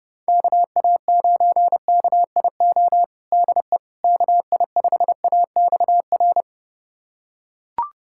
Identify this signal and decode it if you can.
Morse code, heard as a keyed tone, about 30 wpm, 700 Hz. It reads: KA9KIO DE KI5AXR
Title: USA Callsign DE Callsign - 6 Characters - 30wpm